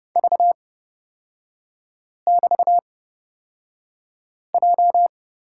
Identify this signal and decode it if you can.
Morse code, heard as a keyed tone, about 30 wpm, spaced as for 5 wpm, 700 Hz. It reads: V<BT>J
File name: Lesson 36e - BT - Three Characters 30wpm